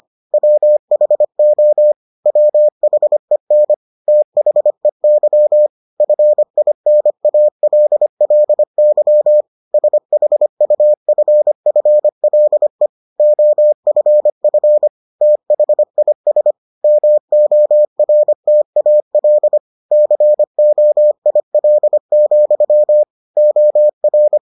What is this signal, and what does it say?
Morse code, heard as a keyed tone, about 25 wpm, 600 Hz. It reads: WHO WHEN THEY FINALLY SHUFFLE OFF THIS MORTAL COIL, OR